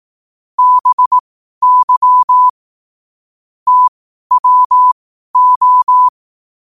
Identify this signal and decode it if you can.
Morse code, heard as a keyed tone, about 18 wpm, spaced as for 12 wpm, 1000 Hz. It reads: BY TWO
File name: qotdfiles / 12wpm